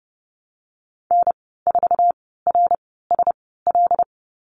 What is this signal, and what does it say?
Morse code, heard as a keyed tone, about 31 wpm, spaced as for 17 wpm, 700 Hz. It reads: N4RSL
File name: Lesson 14g - Number 2 - Call Signs 30wpm